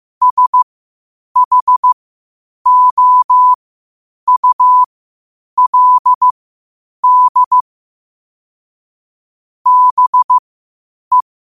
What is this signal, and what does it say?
Morse code, heard as a keyed tone, about 15 wpm, spaced as for 8 wpm, 1000 Hz. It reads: SHOULD BE
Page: qotd-at-8-wpm